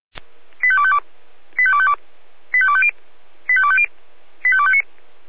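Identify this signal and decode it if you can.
Five-tone selective call (ZVEI), heard as quick runs of five tones, tones of about 70 ms. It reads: 85252 85252 85250 85250 85259
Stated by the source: Five-tone ZVEI selective call, code 85340, received by SDR.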